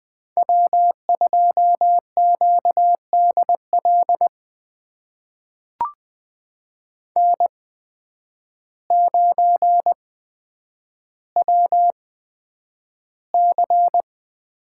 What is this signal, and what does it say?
Morse code, heard as a keyed tone, about 20 wpm, spaced as for 9 wpm, 700 Hz. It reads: W2QDL N9WC
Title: USA Call Signs - 5 Characters - 8x Character Spacing - 20wpm